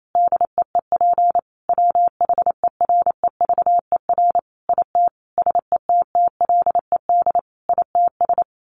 Morse code, heard as a keyed tone, about 28 wpm, 700 Hz. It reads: DEEP WHEREVER IT SETTLED ITS
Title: A Sound Of Thunder 28wpm